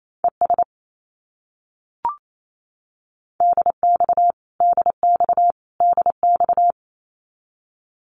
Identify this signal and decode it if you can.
Morse code, heard as a keyed tone, about 28 wpm, 700 Hz. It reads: ES DX DX DX
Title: Abbreviations Repeated 3x - 28wpm